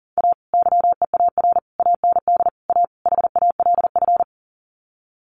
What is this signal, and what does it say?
Morse code, heard as a keyed tone, about 40 wpm, 700 Hz. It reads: A YEAR AND A HALF
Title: Sets of 5 Words 40wpm